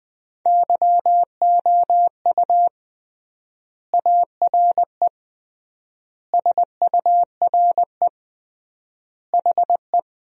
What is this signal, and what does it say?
Morse code, heard as a keyed tone, about 20 wpm, 700 Hz. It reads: YOU ARE SURE HE
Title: 3x Word-Spacing - Sentences from Top 300 Words - 20wpm